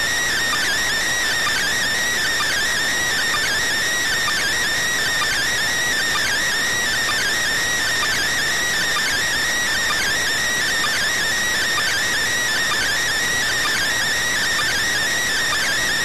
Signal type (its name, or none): two-tone teleprinter signal